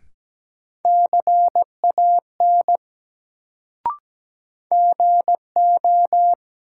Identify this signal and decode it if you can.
Morse code, heard as a keyed tone, about 17 wpm, 700 Hz. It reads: CAN GO
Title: Sentences from Top 100 Words - New Word Drill - 17wpm